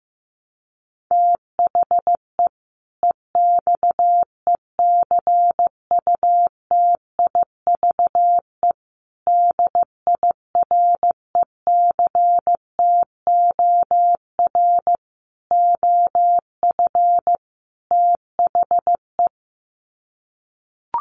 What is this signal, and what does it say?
Morse code, heard as a keyed tone, about 15 wpm, 700 Hz. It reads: THE EXECUTIVE DIRECTOR OF THE
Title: Sets of 5 Words Encore 15wpm